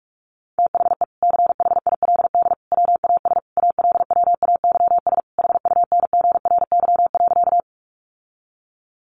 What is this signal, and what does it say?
Morse code, heard as a keyed tone, about 44 wpm, 700 Hz. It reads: THE CHILD WAS ALWAYS HUNGRY.
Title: Sentences from Top 1400 Words 45wpm